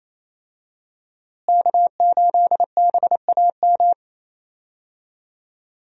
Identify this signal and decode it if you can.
Morse code, heard as a keyed tone, about 27 wpm, 700 Hz. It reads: K8BAM
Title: USA Call Signs - 5 Characters - Repeated 2x - 28wpm